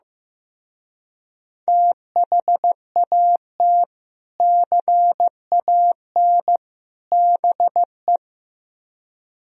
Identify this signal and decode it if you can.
Morse code, heard as a keyed tone, about 15 wpm, 700 Hz. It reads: THAT CAN BE
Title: Speed-Racing - Sets of 3 Words 15wpm